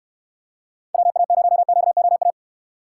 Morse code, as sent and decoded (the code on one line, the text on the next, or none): .-- .- ----- --.- --- .-
WA0QOA